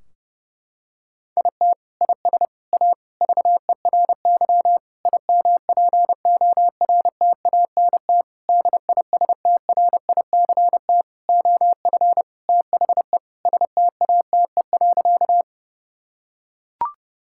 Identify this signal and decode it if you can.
Morse code, heard as a keyed tone, about 32 wpm, 700 Hz. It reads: IT IS A VERY IMPORTANT DISTRICT OF THE STATE.
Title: Sentences from Top 1600 Words 30wpm